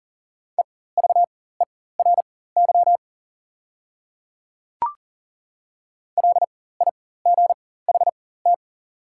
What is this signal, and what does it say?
Morse code, heard as a keyed tone, about 40 wpm, spaced as for 19 wpm, 690 Hz. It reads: EVERY LIGHT
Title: 4x Character Spacing - Top 200 Words - 40wpm